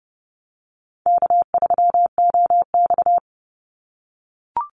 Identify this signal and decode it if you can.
Morse code, heard as a keyed tone, about 30 wpm, 700 Hz. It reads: K3OX
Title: Easy Call Signs - 30wpm